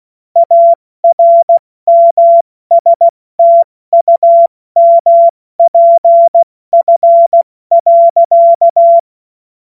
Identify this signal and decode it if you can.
Morse code, heard as a keyed tone, about 16 wpm, 680 Hz. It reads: ARMSTUMPF.
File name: Morsekurs / Witze / Witz 17. Mai 2022 70 F80